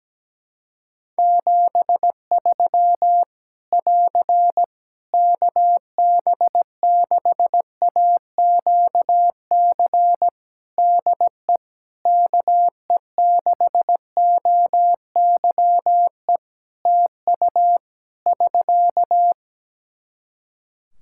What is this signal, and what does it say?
Morse code, heard as a keyed tone, about 17 wpm, 700 Hz. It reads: 73 <AR> KB6AQC DE KE6OYE TU <SK>
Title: Practice QSO - Exchange 4 - Ending - 17wpm